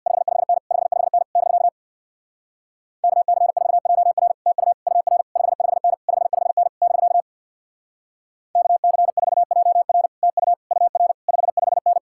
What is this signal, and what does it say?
Morse code, heard as a keyed tone, about 64 wpm, 700 Hz. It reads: 55N 55N <BK> KC4JR TU UR 55N 55N <BK> KC4JR TU UR 55N